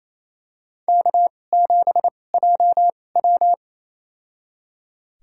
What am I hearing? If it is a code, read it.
Morse code, heard as a keyed tone, about 28 wpm, spaced as for 20 wpm, 700 Hz. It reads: K7JW
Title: USA Call Signs - 2x Character Spacing - 28wpm